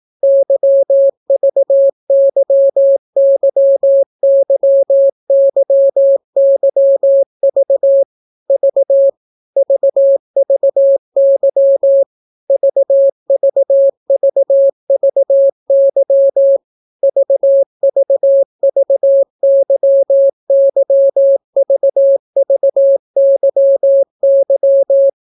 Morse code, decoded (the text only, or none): YVYYYYYV V VVY VVVVY VVVYYVVYY